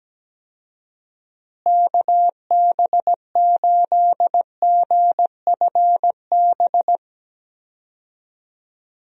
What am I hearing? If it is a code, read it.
Morse code, heard as a keyed tone, about 17 wpm, 700 Hz. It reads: KB8GFB
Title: USA Call Signs - 6 Characters - Repeated 2x - 17wpm